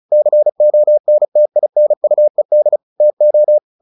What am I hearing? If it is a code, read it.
Morse code, heard as a keyed tone, about 34 wpm, 600 Hz. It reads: CONTINUED TO
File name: the old man and the sea / the old man and the sea 1 35wpm